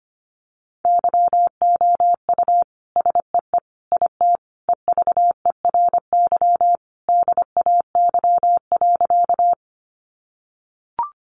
Morse code, heard as a keyed tone, about 25 wpm, 700 Hz. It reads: YOU SEE IT EVERY DAY.
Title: Sentences from Top 200 Words 25wpm